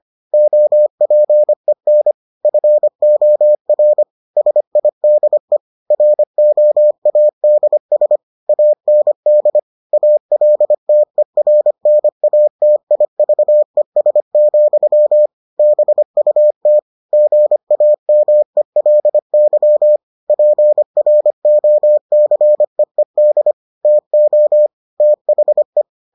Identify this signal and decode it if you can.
Morse code, heard as a keyed tone, about 25 wpm, 600 Hz. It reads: OPEN FOR SIDE ROADS AND ALTERNATIVES, BUT GAMELY PROCEED TO THE